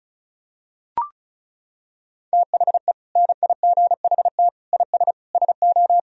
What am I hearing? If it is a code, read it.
Morse code, heard as a keyed tone, about 36 wpm, 700 Hz. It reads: THE NIGHT IS SO
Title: Sentences from Top 500 Words and Abbreviations 35wpm